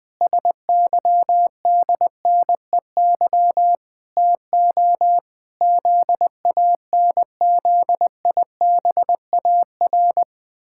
Morse code, heard as a keyed tone, about 20 wpm, 700 Hz. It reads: SYDNEY TO ZANZIBAR
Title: Pangram Challenge v2 - 20wpm